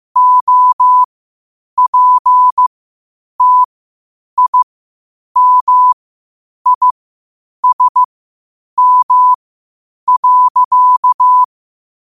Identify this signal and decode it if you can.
Morse code, heard as a keyed tone, about 15 wpm, spaced as for 8 wpm, 1000 Hz. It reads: OPTIMISM.
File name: qotdfiles / 8wpm